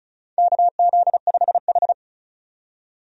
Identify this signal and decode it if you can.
Morse code, heard as a keyed tone, about 36 wpm, 700 Hz. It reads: KZ5H
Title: USA Call Signs - 4 Characters - 35wpm